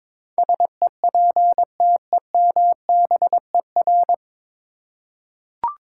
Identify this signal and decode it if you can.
Morse code, heard as a keyed tone, about 22 wpm, 700 Hz. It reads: SEPTEMBER
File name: practice / 22wpm / Nine Letter Words - 22wpm